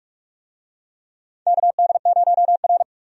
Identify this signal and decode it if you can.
Morse code, heard as a keyed tone, about 43 wpm, 700 Hz. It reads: KD0R
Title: USA Call Signs - 4 Characters - 45wpm